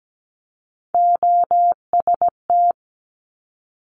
Morse code, heard as a keyed tone, about 17 wpm, 700 Hz. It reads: OST
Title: Multi-letter Phonemes 17wpm